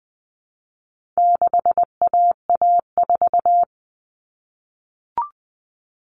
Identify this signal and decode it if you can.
Morse code, heard as a keyed tone, about 20 wpm, 700 Hz. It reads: 6AA4
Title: All Cut Numbers - 4-Digit 20wpm